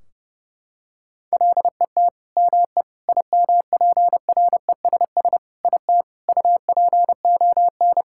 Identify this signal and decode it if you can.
Morse code, heard as a keyed tone, about 28 wpm, 700 Hz. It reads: LET ME IMPRESS IT UPON
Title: Sentences from Top 1600 Words 30wpm